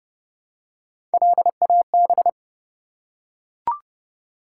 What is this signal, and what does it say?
Morse code, heard as a keyed tone, about 28 wpm, 700 Hz. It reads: LAB
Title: Three Letter Words - 30wpm